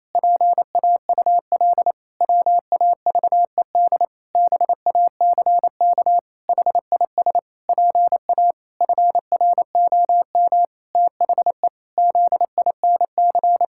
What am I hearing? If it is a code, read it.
Morse code, heard as a keyed tone, about 28 wpm, 700 Hz. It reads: PAUL WAVED BACK HIS PA FROM THE ZINC
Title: Pangram Challenge v2 - 28wpm